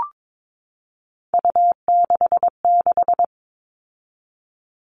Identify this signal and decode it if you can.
Morse code, heard as a keyed tone, about 22 wpm, 700 Hz. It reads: U66
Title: All Cut Numbers - 3-Digit 22wpm